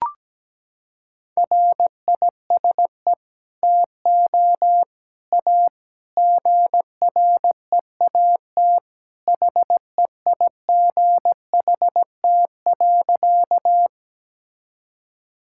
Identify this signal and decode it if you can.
Morse code, heard as a keyed tone, about 17 wpm, 700 Hz. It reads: RISE TO A GREAT HEIGHT.
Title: Sentences from Top 1600 Words 17wpm